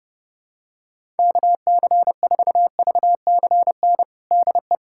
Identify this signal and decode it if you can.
Morse code, heard as a keyed tone, about 32 wpm, 700 Hz. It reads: KC4VCN DE
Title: USA Callsign DE Callsign - 6 Characters - 30wpm